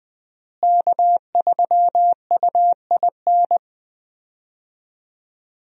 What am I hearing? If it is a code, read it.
Morse code, heard as a keyed tone, about 21 wpm, 700 Hz. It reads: K3UIN
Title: USA Call Signs - 5 Characters - Repeated 3x - 20wpm